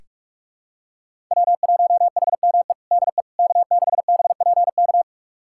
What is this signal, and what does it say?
Morse code, heard as a keyed tone, about 46 wpm, 700 Hz. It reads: W1HME DE K6BPK
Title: USA Callsign DE Callsign - 5 Characters - 45wpm